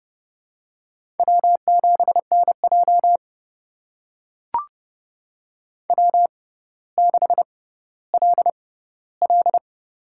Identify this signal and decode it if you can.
Morse code, heard as a keyed tone, about 30 wpm, spaced as for 15 wpm, 700 Hz. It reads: W7NJ W6LL
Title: USA Call Signs - 6x Character Spacing - 30wpm